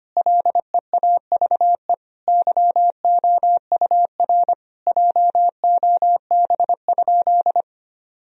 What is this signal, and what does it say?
Morse code, heard as a keyed tone, about 25 wpm, 700 Hz. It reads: LEAVE YOUR JOB?
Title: Sentences from Top 1000 Words - Review Part 3 - 25wpm